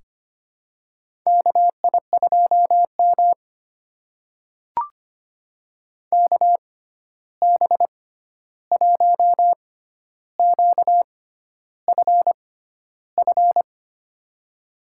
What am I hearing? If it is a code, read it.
Morse code, heard as a keyed tone, about 25 wpm, spaced as for 11 wpm, 700 Hz. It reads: KI2M KB1QFF
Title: USA Call Signs - 6x Character Spacing - 25wpm